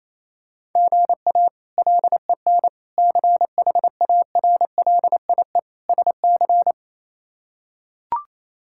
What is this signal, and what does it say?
Morse code, heard as a keyed tone, about 28 wpm, 700 Hz. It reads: GA LEN CHARLIE SC
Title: K1USN SST QSO Practice - 28wpm